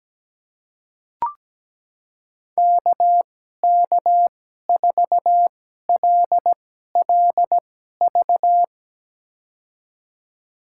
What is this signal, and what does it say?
Morse code, heard as a keyed tone, about 17 wpm, spaced as for 12 wpm, 700 Hz. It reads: KK4LLV
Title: USA Call Signs - 6 Characters - 2x Character Spacing - 17wpm